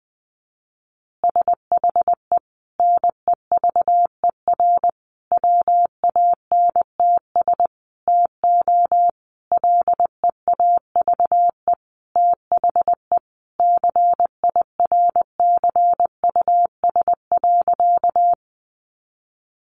Morse code, heard as a keyed tone, about 20 wpm, 700 Hz. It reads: SHE NEVER WANTS TO LEAVE THE CIRCUS.
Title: Easy Phrases 20wpm